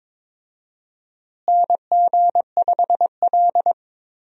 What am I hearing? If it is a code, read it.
Morse code, heard as a keyed tone, about 22 wpm, 700 Hz. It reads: NG5L
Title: USA Call Signs - 4 Characters - Repeated 2x - 22wpm